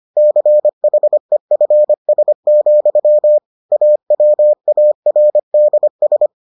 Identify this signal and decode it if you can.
Morse code, heard as a keyed tone, about 25 wpm, 600 Hz. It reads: CHEFS, AWARDS